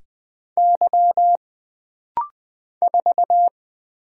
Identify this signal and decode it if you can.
Morse code, heard as a keyed tone, about 20 wpm, 700 Hz. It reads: Y 4
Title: Single Letter-Number Alt - Rapid-Fire 20wpm